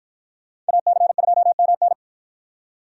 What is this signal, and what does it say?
Morse code, heard as a keyed tone, about 50 wpm, 700 Hz. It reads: AC2MN